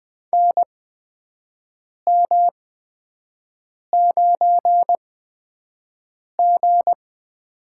Morse code, heard as a keyed tone, about 20 wpm, spaced as for 5 wpm, 700 Hz. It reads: NM9G
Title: USA Call Signs - 4 Characters - 8x Character Spacing - 20wpm